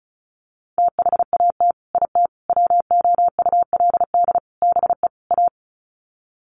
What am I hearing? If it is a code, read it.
Morse code, heard as a keyed tone, about 34 wpm, 700 Hz. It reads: THAT IT WOULD BE A
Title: Sets of 5 Words Encore2 35wpm